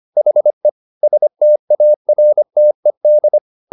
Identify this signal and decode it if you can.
Morse code, heard as a keyed tone, about 25 wpm, 600 Hz. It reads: HE STARTED